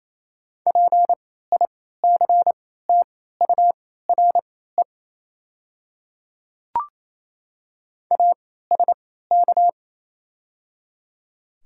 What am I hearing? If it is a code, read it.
Morse code, heard as a keyed tone, about 28 wpm, spaced as for 16 wpm, 700 Hz. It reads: PICTURE ASK
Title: 3x Character Spacing - Top 200 Words - 28wpm